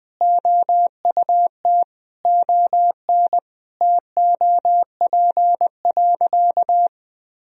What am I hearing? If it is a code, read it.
Morse code, heard as a keyed tone, about 20 wpm, 700 Hz. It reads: OUT ON TOP.
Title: Sentences from Top 1000 Words - Review Part 2 - 20wpm